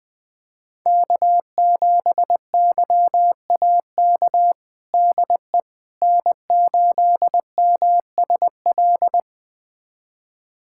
Morse code, heard as a keyed tone, about 20 wpm, 700 Hz. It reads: K7YAK DE N8MSL
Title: USA Callsign DE Callsign - 5 Characters - 20wpm